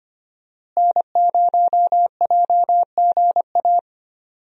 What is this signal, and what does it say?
Morse code, heard as a keyed tone, about 25 wpm, 700 Hz. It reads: N0JGA
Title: USA Call Signs - 5 Characters - Repeated 3x - 25wpm